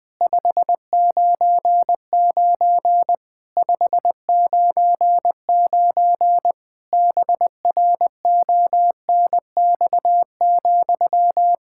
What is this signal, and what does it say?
Morse code, heard as a keyed tone, about 20 wpm, 700 Hz. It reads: 599 599 BRONX,
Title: Practice QSO - Exchange 1 - Beginning - 20wpm